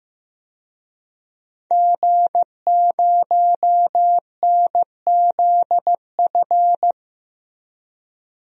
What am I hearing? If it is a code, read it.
Morse code, heard as a keyed tone, about 15 wpm, 700 Hz. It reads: G0NZF
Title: DXCC-100 Call Signs - 5 Characters - 15wpm